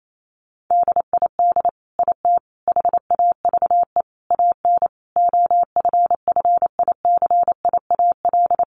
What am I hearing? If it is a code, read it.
Morse code, heard as a keyed tone, about 28 wpm, 700 Hz. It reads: DID IT HAVE AN OFFICIAL
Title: Sentences from Top 1200 Words 28wpm